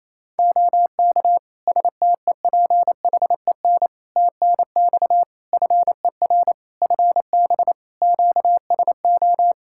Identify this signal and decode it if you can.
Morse code, heard as a keyed tone, about 28 wpm, 700 Hz. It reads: OK STEPHEN TNX FER FB QSO
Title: K1USN SST QSO Practice 28wpm